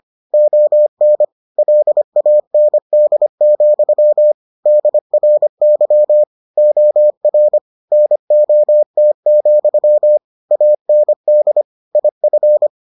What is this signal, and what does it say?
Morse code, heard as a keyed tone, about 25 wpm, 600 Hz. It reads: ON LAND, DRY OR NOT, AND IF